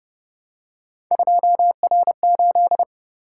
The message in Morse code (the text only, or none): 2R8